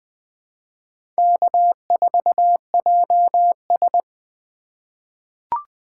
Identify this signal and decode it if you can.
Morse code, heard as a keyed tone, about 21 wpm, 700 Hz. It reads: K4JS
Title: USA Call Signs - 4 Characters - 3x Character Spacing - 20wpm